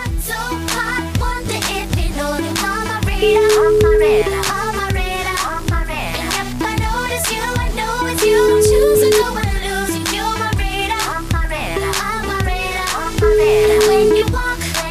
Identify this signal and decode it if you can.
Ringback tone (Europe, 425 Hz), with noise.